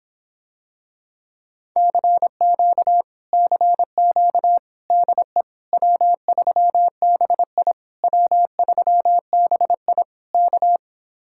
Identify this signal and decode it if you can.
Morse code, heard as a keyed tone, about 26 wpm, 700 Hz. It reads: CQ CQ DE W3BI W3BI K